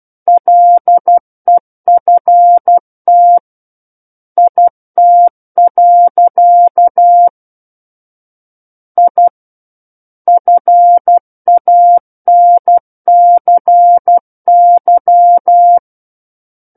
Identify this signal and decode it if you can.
Morse code, heard as a keyed tone, about 12 wpm, 700 Hz. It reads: LEFT IT. I FANCY
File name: Book One / 03-Chapter 3 - 12 WPM